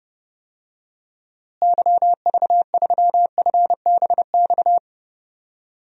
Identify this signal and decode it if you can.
Morse code, heard as a keyed tone, about 30 wpm, 700 Hz. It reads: YV3FBX
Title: DXCC-100 Call Signs - 6 Characters - Repeated 3x - 30wpm